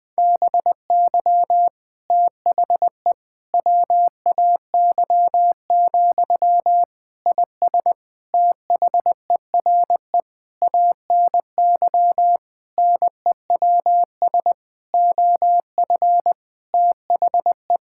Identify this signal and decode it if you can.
Morse code, heard as a keyed tone, about 20 wpm, 700 Hz. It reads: BY THE WAY, IS THERE ANY NEWS OF THE
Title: Sentences from Top 1600 Words 20wpm